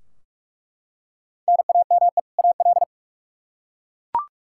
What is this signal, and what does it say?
Morse code, heard as a keyed tone, about 41 wpm, 700 Hz. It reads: NAME AL
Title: K1USN SST QSO Practice 45wpm